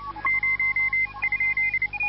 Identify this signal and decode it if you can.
Morse code, heard as a keyed tone, about 29 wpm, spaced as for 25 wpm, 2050 Hz. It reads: 0?
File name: forum / s53f- lsb ts900